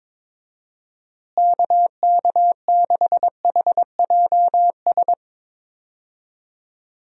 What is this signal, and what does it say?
Morse code, heard as a keyed tone, about 22 wpm, 700 Hz. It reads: KK6HJS
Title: USA Call Signs - 6 Characters - Repeated 2x - 22wpm